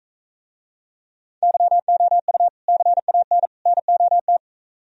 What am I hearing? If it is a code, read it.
Morse code, heard as a keyed tone, about 45 wpm, 700 Hz. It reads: YOU CAN NOT